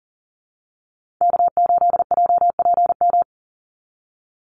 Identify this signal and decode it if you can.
Morse code, heard as a keyed tone, about 40 wpm, 700 Hz. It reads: K8JPM
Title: Call Signs Repeated 3x - 40wpm